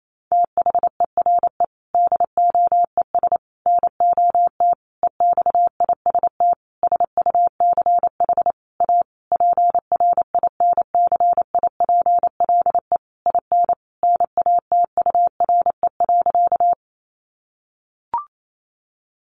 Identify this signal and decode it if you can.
Morse code, heard as a keyed tone, about 28 wpm, 700 Hz. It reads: THERE DOES NOT EXIST SUCH A PRINCIPLE IN NATURE.
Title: Sentences from Top 1400 Words 28wpm